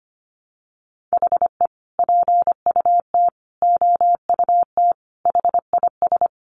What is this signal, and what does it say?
Morse code, heard as a keyed tone, about 25 wpm, 700 Hz. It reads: HE PUT OUT HIS
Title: Sentences from Top 800 Words 25wpm